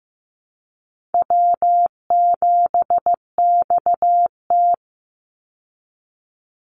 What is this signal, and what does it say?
Morse code, heard as a keyed tone, about 15 wpm, 700 Hz. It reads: W7XT
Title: Easy Call Signs Repeated 3x - 15wpm